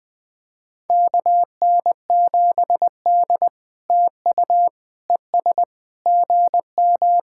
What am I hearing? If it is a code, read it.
Morse code, heard as a keyed tone, about 20 wpm, 700 Hz. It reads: KN7D TU ES GM
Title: POTA QSO Practice - 20wpm